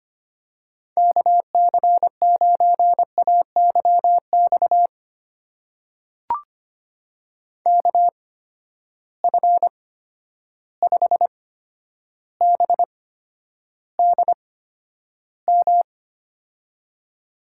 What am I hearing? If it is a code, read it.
Morse code, heard as a keyed tone, about 25 wpm, spaced as for 11 wpm, 700 Hz. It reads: KC9AYX KF5BDM
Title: USA Call Signs - 6 Characters - 8x Character Spacing - 25wpm